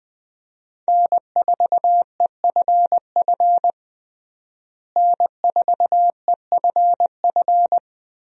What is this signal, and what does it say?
Morse code, heard as a keyed tone, about 20 wpm, 700 Hz. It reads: N4EFF N4EFF
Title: USA Call Signs - 5 Characters - Repeated 3x - 20wpm